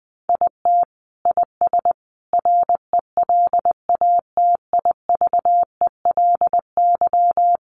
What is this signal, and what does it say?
Morse code, heard as a keyed tone, about 20 wpm, 700 Hz. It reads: IT IS RELATIVELY
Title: Sentences from Top 1100 Words 20wpm